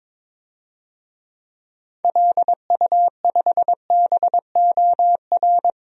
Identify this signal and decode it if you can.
Morse code, heard as a keyed tone, about 22 wpm, 700 Hz. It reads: LU5BOR
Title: DXCC-100 Call Signs - 6 Characters - 22wpm